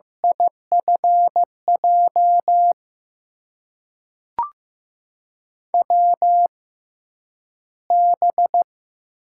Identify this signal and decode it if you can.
Morse code, heard as a keyed tone, about 15 wpm, spaced as for 9 wpm, 700 Hz. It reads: IFJ WB
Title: USA Call Signs - 6 Characters - 6x Character Spacing - 15wpm